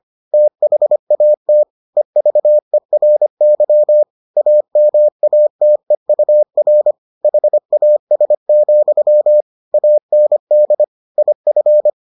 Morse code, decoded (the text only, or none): THAT EVERY AMATEUR HAS, AND IF